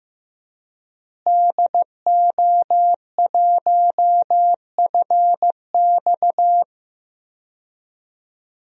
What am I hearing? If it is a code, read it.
Morse code, heard as a keyed tone, about 15 wpm, 700 Hz. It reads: DO1FX
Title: DXCC-100 Call Signs - 5 Characters - Repeated 3x - 15wpm